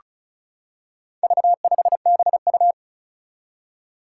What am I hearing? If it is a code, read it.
Morse code, heard as a keyed tone, about 36 wpm, 700 Hz. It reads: V5BU
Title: Four Challenging Letter-Number Combinations 35wpm